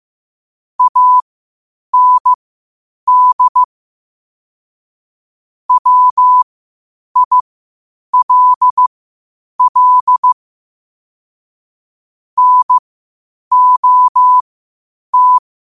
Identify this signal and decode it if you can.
Morse code, heard as a keyed tone, about 15 wpm, spaced as for 8 wpm, 1000 Hz. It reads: AND WILL NOT
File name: qotdfiles / 8wpm